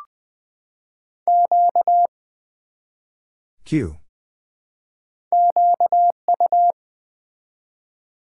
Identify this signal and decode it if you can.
Morse code, heard as a keyed tone, about 20 wpm, 700 Hz. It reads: Q QU
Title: Build A Word - 1 Letter At A Time - 5-Letter Words - 20wpm